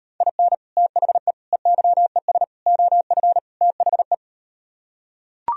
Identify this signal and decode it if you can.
Morse code, heard as a keyed tone, about 38 wpm, 700 Hz. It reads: IN THE EYES OF THE